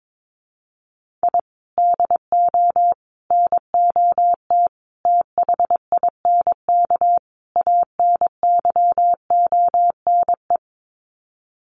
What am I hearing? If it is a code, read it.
Morse code, heard as a keyed tone, about 22 wpm, 700 Hz. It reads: I DO NOT THINK ANYONE
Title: Sets of 5 Words Encore 22wpm